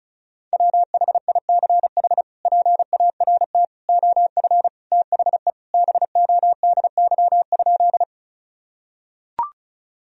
Morse code, heard as a keyed tone, about 36 wpm, 700 Hz. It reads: WHICH PART OF THE BODY?
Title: Sentences from Top 1000 Words - Review Part 2 - 35wpm